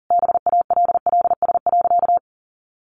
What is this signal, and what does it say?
Morse code, heard as a keyed tone, about 40 wpm, 700 Hz. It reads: BALLS.
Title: Easy Phrases 40wpm